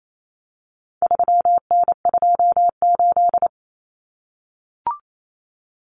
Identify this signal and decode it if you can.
Morse code, heard as a keyed tone, about 28 wpm, 700 Hz. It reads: 3N28
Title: Common Cut Numbers - 4-Digit 28wpm